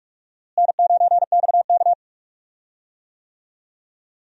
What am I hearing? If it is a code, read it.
Morse code, heard as a keyed tone, about 44 wpm, 700 Hz. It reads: N9XK
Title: USA Call Signs - 4 Characters - Repeated 3x - 45wpm